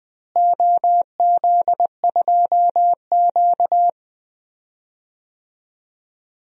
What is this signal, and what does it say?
Morse code, heard as a keyed tone, about 20 wpm, 700 Hz. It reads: OZ2Q
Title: DXCC-100 Call Signs - 4 Characters - Repeated 3x - 20wpm